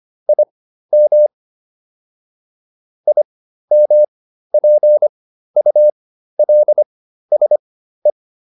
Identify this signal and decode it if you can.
Morse code, heard as a keyed tone, about 25 wpm, spaced as for 13 wpm, 600 Hz. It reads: IM IMPULSE